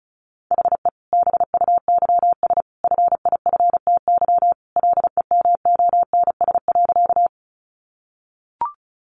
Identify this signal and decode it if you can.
Morse code, heard as a keyed tone, about 34 wpm, 700 Hz. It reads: HE BUYS FIFTY LEMONS.
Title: Additional Easy Phrases 35wpm